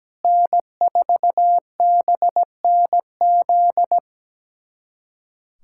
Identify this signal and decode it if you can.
Morse code, heard as a keyed tone, about 17 wpm, 700 Hz. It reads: N4BNZ
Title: USA Callsign DE Callsign - 5 Characters - 17wpm